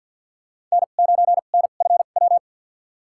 Morse code, heard as a keyed tone, about 50 wpm, spaced as for 36 wpm, 700 Hz. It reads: N9NFW
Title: USA Call Signs - 5 Characters - 2x Character Spacing - 50wpm